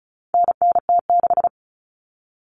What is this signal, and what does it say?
Morse code, heard as a keyed tone, about 34 wpm, 700 Hz. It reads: NNT6
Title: Common Cut Numbers - 4-Digit 35wpm